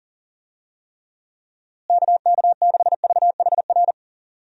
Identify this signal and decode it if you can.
Morse code, heard as a keyed tone, about 37 wpm, 700 Hz. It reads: KK6VHR
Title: USA Call Signs - 6 Characters - Repeated 3x - 40wpm